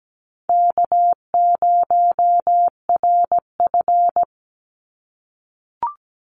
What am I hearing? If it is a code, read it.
Morse code, heard as a keyed tone, about 17 wpm, 700 Hz. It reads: K0RF
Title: Call Signs - 17wpm